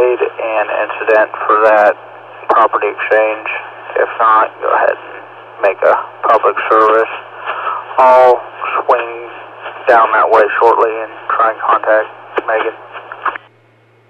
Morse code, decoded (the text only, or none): none